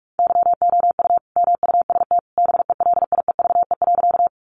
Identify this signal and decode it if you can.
Morse code, heard as a keyed tone, about 44 wpm, 700 Hz. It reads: YOU MUST BELIEVE.
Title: Sentences from Top 1000 Words 45wpm